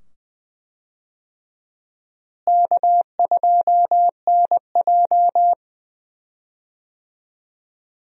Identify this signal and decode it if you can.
Morse code, heard as a keyed tone, about 20 wpm, 700 Hz. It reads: K2NJ
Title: USA Call Signs - 4 Characters - Repeated 2x - 20wpm